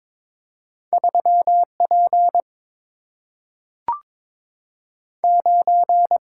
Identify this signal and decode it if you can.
Morse code, heard as a keyed tone, about 22 wpm, 700 Hz. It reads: 3 P 9
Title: Single-Letter Single-Number 22wpm